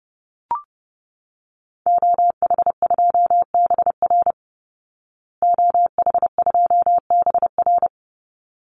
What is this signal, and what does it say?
Morse code, heard as a keyed tone, about 30 wpm, 700 Hz. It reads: OH2BR OH2BR
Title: Call Signs Repeated 3x - 30wpm